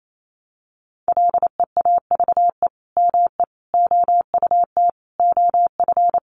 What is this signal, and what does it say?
Morse code, heard as a keyed tone, about 28 wpm, 700 Hz. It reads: LEAVE ME OUT OF
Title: Sentences from Top 400 Words 28wpm